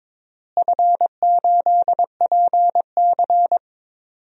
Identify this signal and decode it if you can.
Morse code, heard as a keyed tone, about 22 wpm, 700 Hz. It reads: F8PC
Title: DXCC-100 Call Signs - 4 Characters - Repeated 3x - 22wpm